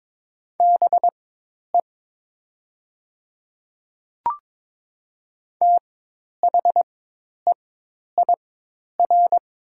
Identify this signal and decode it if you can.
Morse code, heard as a keyed tone, about 22 wpm, spaced as for 10 wpm, 700 Hz. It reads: BE THEIR
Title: 4x Character Spacing - Top 100 Words - 22wpm